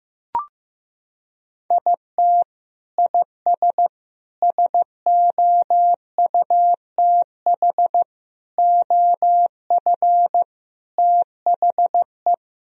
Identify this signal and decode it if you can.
Morse code, heard as a keyed tone, about 15 wpm, 700 Hz. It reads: IT IS SOUTH OF THE
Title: Sentences from Top 1000 Words - Review Part 4 - 15wpm